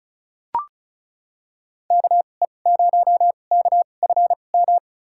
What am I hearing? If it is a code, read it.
Morse code, heard as a keyed tone, about 36 wpm, spaced as for 25 wpm, 700 Hz. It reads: KE0KFM
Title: USA Call Signs - 6 Characters - 2x Character Spacing - 35wpm